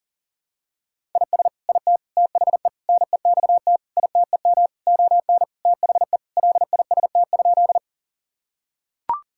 Morse code, heard as a keyed tone, about 40 wpm, 700 Hz. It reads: IS IT THE NEXT ITEM ON THE LIST?
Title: Sentences from Top 1000 Words - Review Part 6 - 40wpm